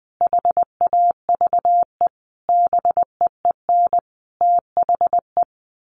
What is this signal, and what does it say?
Morse code, heard as a keyed tone, about 20 wpm, 700 Hz. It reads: HAVE BEEN THE
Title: Sentences from Top 1300 Words 20wpm